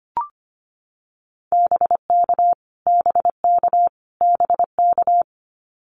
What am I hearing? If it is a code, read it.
Morse code, heard as a keyed tone, about 25 wpm, 700 Hz. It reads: BK BK BK
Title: Abbreviations Repeated 3x - 25wpm